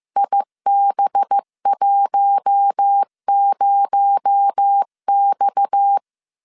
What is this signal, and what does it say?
Morse code, heard as a keyed tone, about 15 wpm, 800 Hz. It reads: IB10X